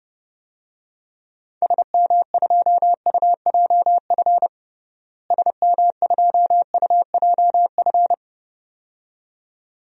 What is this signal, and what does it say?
Morse code, heard as a keyed tone, about 30 wpm, 700 Hz. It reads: SM2UJF SM2UJF
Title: DXCC-100 Call Signs - 6 Characters - Repeated 2x - 30wpm